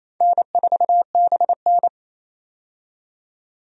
Morse code, heard as a keyed tone, about 28 wpm, 700 Hz. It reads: N4BN
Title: USA Call Signs - 4 Characters - 28wpm